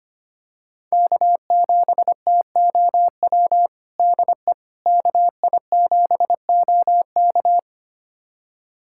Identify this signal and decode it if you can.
Morse code, heard as a keyed tone, about 25 wpm, 700 Hz. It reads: K7TOW DE KI7OK
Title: USA Callsign DE Callsign - 5 Characters - 25wpm